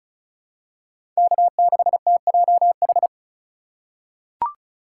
Morse code, heard as a keyed tone, about 36 wpm, 700 Hz. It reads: K6TJH
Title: USA Call Signs - 5 Characters - 3x Character Spacing - 35wpm